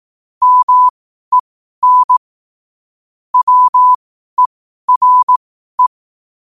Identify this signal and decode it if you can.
Morse code, heard as a keyed tone, about 18 wpm, spaced as for 12 wpm, 1000 Hz. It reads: MEN WERE
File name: qotdfiles / 12wpm